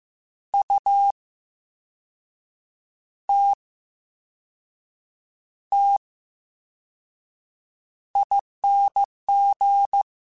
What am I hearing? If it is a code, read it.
Morse code, heard as a keyed tone, about 15 wpm, spaced as for 5 wpm, 800 Hz. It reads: UTTING
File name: Special INGs-mp3s / Special INGs-mp3s / VerySlowToFaster Ing @13